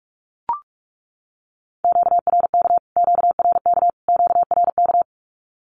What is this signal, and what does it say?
Morse code, heard as a keyed tone, about 44 wpm, 700 Hz. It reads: QRK QRK QRK
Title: Q-Codes Repeated 3x 45wpm